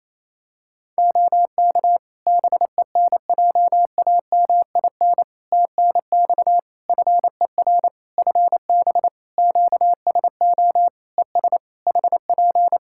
Morse code, heard as a keyed tone, about 27 wpm, 700 Hz. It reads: OK BENJAMIN TNX FER FB QSO ES HP
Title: Practice QSO - Exchange 4 - Ending - 28wpm